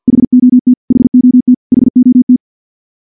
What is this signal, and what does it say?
Morse code, heard as a keyed tone, about 46 wpm, 250 Hz. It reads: HOT HOT HOT
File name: morse / morse-training-50WPM